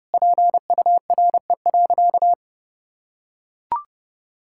Morse code, heard as a keyed tone, about 30 wpm, 700 Hz. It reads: PURE.
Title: Sentences from Top 1600 Words 30wpm